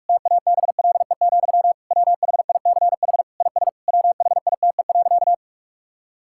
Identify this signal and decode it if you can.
Morse code, heard as a keyed tone, about 41 wpm, 700 Hz. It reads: TABLE, WHICH IS WHITE.
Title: Sentences from Top 1600 Words 45wpm